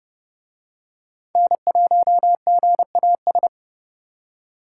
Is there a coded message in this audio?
Yes: Morse code from 1.4 to 3.5 s, heard as a keyed tone, about 29 wpm, 700 Hz. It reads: N1GAS